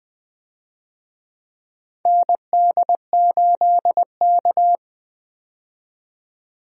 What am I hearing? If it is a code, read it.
Morse code, heard as a keyed tone, about 20 wpm, 700 Hz. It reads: ND8K
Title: USA Call Signs - 4 Characters - 20wpm